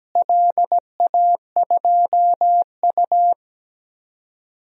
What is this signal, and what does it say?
Morse code, heard as a keyed tone, about 17 wpm, 700 Hz. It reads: LA2U
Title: DXCC-100 Call Signs - 4 Characters - Repeated 2x - 17wpm